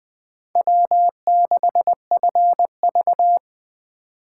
Morse code, heard as a keyed tone, about 24 wpm, 700 Hz. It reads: W6FV
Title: Speed Racing - USA 1x2 Call Signs 20wpm